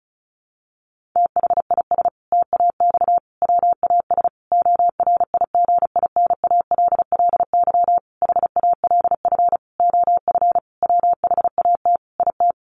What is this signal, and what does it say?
Morse code, heard as a keyed tone, about 34 wpm, 700 Hz. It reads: THIS TAX WAS ORIGINALLY HALF OF WHAT IT
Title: Sentences from Top 1500 Words 35wpm